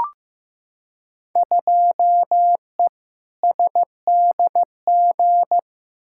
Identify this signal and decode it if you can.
Morse code, heard as a keyed tone, about 15 wpm, 700 Hz. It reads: 2E SDG
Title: Field Day Exchanges - 15wpm